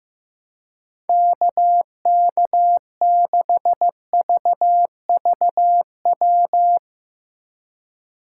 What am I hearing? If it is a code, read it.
Morse code, heard as a keyed tone, about 15 wpm, 700 Hz. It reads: KK6VVW
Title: USA Call Signs - 3x Character Spacing - 15wpm